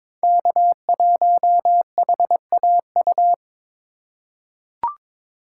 Morse code, heard as a keyed tone, about 22 wpm, 700 Hz. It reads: K1HAU
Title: USA Call Signs - 5 Characters - 4x Character Spacing - 22wpm